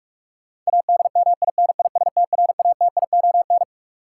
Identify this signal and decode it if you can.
Morse code, heard as a keyed tone, about 46 wpm, 700 Hz. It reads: ADMINISTRATION
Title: Sets of 5 Words - New Word Drill - 45wpm